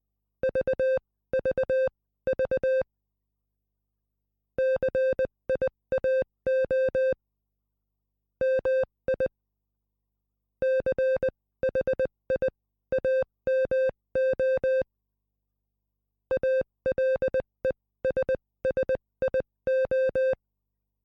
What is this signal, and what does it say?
Morse code, heard as a keyed tone, about 20 wpm, spaced as for 16 wpm, 520 Hz. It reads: VVV CIAO MI CHIAMO ALESSIO